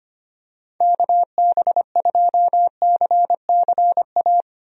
Morse code, heard as a keyed tone, about 25 wpm, 700 Hz. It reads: KB2CCA
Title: USA Call Signs - 6 Characters - 25wpm